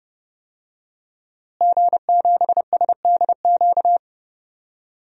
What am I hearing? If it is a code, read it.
Morse code, heard as a keyed tone, about 30 wpm, 700 Hz. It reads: G7SDQ
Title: DXCC-100 Call Signs - 5 Characters - Repeated 3x - 30wpm